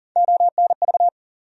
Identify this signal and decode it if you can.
Morse code, heard as a keyed tone, about 40 wpm, 700 Hz. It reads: ONV
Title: USA Call Sign Postfixes - 3 Letters 40wpm